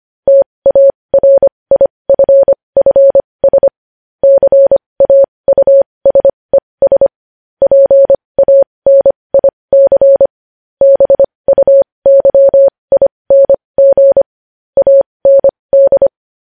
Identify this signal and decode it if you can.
Morse code, heard as a keyed tone, about 25 wpm, spaced as for 20 wpm, 550 Hz. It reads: TARIFFS CAUSES PANIC BUYING AND
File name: podcast / News headlines-1x-25x20